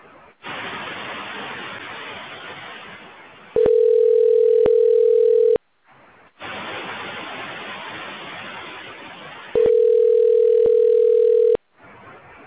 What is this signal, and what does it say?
Ringback tone (North America), 440 + 480 Hz.